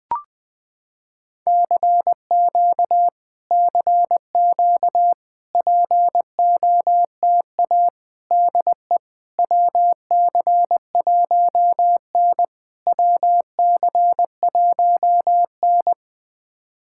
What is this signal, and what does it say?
Morse code, heard as a keyed tone, about 20 wpm, 690 Hz. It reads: CQ CQ POTA DE WC1N WC1N
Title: POTA QSO Practice - 20wpm